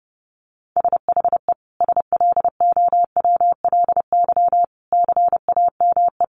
Morse code, heard as a keyed tone, about 30 wpm, 700 Hz. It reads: SHE SLOWLY CAME
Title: Sentences from Top 1100 Words 30wpm